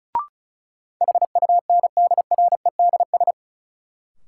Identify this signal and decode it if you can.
Morse code, heard as a keyed tone, about 36 wpm, 700 Hz. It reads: HUNDREDS
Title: Sets of 4 Words - New Word Drill - 35wpm